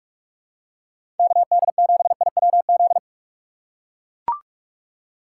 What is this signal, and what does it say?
Morse code, heard as a keyed tone, about 42 wpm, 700 Hz. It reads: KD7IWZ
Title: USA Call Signs - 6 Characters - 6x Character Spacing - 45wpm